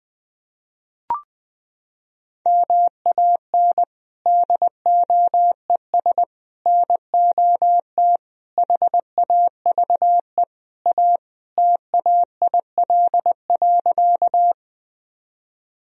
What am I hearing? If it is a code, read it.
Morse code, heard as a keyed tone, about 20 wpm, 700 Hz. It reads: MAN DOES NOT HAVE A TAIL.
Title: Sentences from Top 1000 Words - Review Part 3 - 20wpm